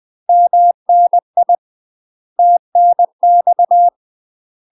Morse code, heard as a keyed tone, about 20 wpm, 700 Hz. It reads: MNI TNX
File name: UK MASTER QSOs TRAINING / 20WPM / UK MASTER QSOs 0004